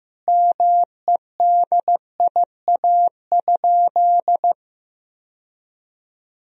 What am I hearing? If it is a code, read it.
Morse code, heard as a keyed tone, about 16 wpm, 700 Hz. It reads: MEDIA?
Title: Sentences from Top 1000 Words - Review Part 4 - 15wpm